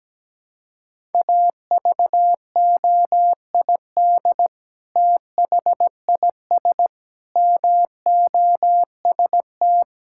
Morse code, heard as a keyed tone, about 17 wpm, 700 Hz. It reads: AVOID THIS MOST
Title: Sentences from Top 1000 Words - Review Part 6 - 17wpm